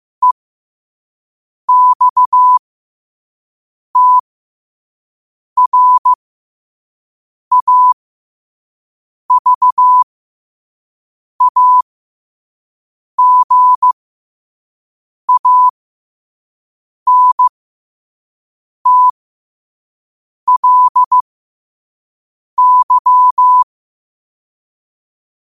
Morse code, heard as a keyed tone, about 15 wpm, spaced as for 5 wpm, 1000 Hz. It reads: EXTRAVAGANTLY
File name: qotdfiles / 5wpm